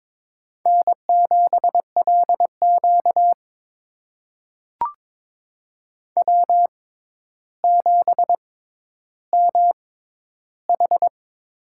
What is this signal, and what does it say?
Morse code, heard as a keyed tone, about 22 wpm, spaced as for 11 wpm, 700 Hz. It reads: N7LQ W7MH
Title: USA Call Signs - 4 Characters - 6x Character Spacing - 22wpm